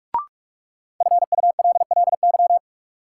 Morse code, heard as a keyed tone, about 46 wpm, 700 Hz. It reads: FULLY
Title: Sentences from Top 1300 Words - New Word Drill - 45wpm